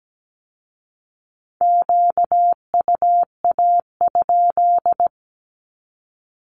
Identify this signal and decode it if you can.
Morse code, heard as a keyed tone, about 17 wpm, 700 Hz. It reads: QUA?
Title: Q-Codes with Spoken Meaning - 17wpm